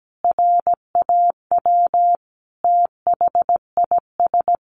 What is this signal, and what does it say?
Morse code, heard as a keyed tone, about 17 wpm, 700 Hz. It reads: RAW THIS
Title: Sentences from Top 1500 Words 17wpm